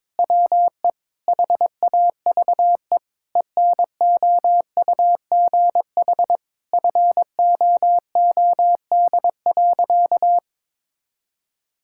Morse code, heard as a keyed tone, about 22 wpm, 700 Hz. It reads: WE HAVE ENOUGH FOOD.
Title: Sentences from Top 1000 Words - Review Part 5 - 22wpm